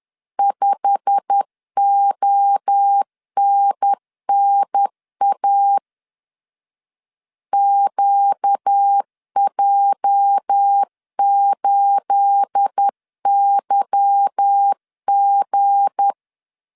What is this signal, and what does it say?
Morse code, heard as a keyed tone, about 10 wpm, 800 Hz. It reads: 5ONNA QJ8YG
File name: CW / DET / 1 9 1500C S A 10wpm 4511